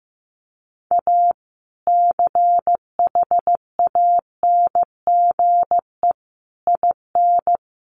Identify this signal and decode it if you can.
Morse code, heard as a keyed tone, about 15 wpm, 700 Hz. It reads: A CHANGE IN
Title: Sentences from Top 1500 Words 15wpm